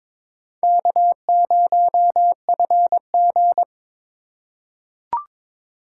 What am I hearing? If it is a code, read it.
Morse code, heard as a keyed tone, about 22 wpm, 700 Hz. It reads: K0FG
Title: USA Call Signs - 4 Characters - 3x Character Spacing - 22wpm